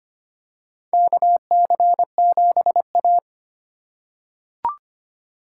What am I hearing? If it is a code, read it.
Morse code, heard as a keyed tone, about 25 wpm, 700 Hz. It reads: KC7A
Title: USA Call Signs - 4 Characters - 6x Character Spacing - 25wpm